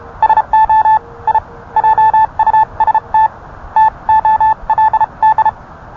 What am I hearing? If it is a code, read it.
Morse code, heard as a keyed tone, about 31 wpm, spaced as for 27 wpm, 870 Hz. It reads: SO I JUST TOLD